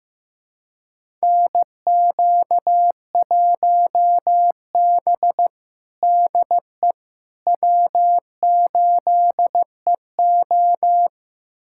Morse code, heard as a keyed tone, about 15 wpm, 700 Hz. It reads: NQ1B DE W8EO
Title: USA Callsign DE Callsign - 4 Characters - 15wpm